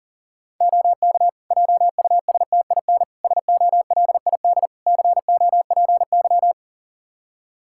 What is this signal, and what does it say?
Morse code, heard as a keyed tone, about 40 wpm, 700 Hz. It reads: OK JUSTIN SOLID COPY
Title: Practice QSO - Exchange 3 - More Info - 40wpm